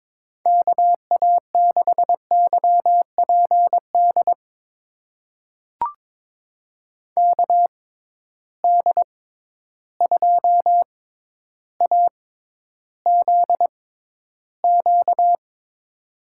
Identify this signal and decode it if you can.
Morse code, heard as a keyed tone, about 21 wpm, spaced as for 11 wpm, 700 Hz. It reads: KA6YPD KD2AZQ
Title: USA Call Signs - 6 Characters - 6x Character Spacing - 22wpm